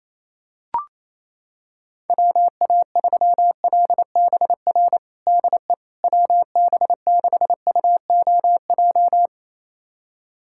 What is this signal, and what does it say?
Morse code, heard as a keyed tone, about 28 wpm, 700 Hz. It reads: WA3LBR DE WB6UOJ
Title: USA Callsign DE Callsign - 6 Characters - 28wpm